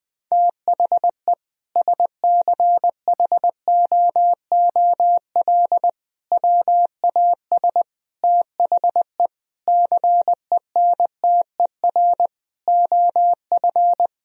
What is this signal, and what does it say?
Morse code, heard as a keyed tone, about 20 wpm, 700 Hz. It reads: THE SCHOOL WAS THE CENTER OF
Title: Sentences from Top 1000 Words - Review Part 2 - 20wpm